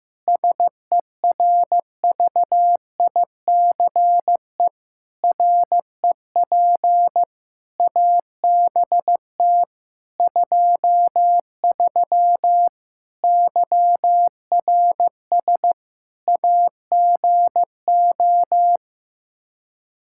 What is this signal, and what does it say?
Morse code, heard as a keyed tone, about 15 wpm, 700 Hz. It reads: SERVICE REP ABT 23 YRS AGO
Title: K1USN SST QSO Practice 15wpm